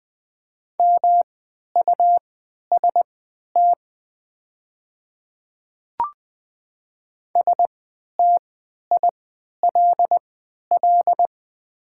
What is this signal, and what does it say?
Morse code, heard as a keyed tone, about 20 wpm, spaced as for 11 wpm, 700 Hz. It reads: MUST STILL
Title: 3x Character Spacing - Top 300 Words - 20wpm